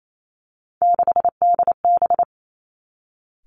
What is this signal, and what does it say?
Morse code, heard as a keyed tone, about 28 wpm, 700 Hz. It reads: 6DB
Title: Three Challenging Letter-Number Combinations 28wpm